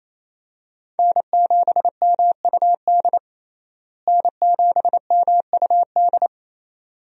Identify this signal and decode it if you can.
Morse code, heard as a keyed tone, about 28 wpm, 700 Hz. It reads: N7MUD N7MUD
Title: USA Call Signs - 5 Characters - Repeated 2x - 28wpm